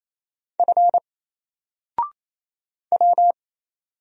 Morse code, heard as a keyed tone, about 28 wpm, 700 Hz. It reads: F W
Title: Single Character (Letter:Number:Punctuation:Prosign) 28wpm